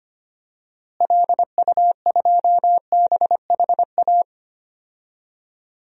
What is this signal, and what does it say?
Morse code, heard as a keyed tone, about 25 wpm, 700 Hz. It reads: LU2BHA
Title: DXCC-100 Call Signs - 6 Characters - Repeated 2x - 25wpm